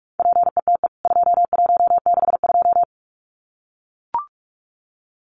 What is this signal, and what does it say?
Morse code, heard as a keyed tone, about 44 wpm, 700 Hz. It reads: PETE 2162
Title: CWT Exchange 45wpm